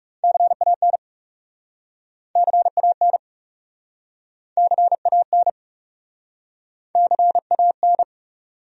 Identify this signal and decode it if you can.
Morse code, heard as a keyed tone, about 41 wpm, 700 Hz. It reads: CAN CAN CAN CAN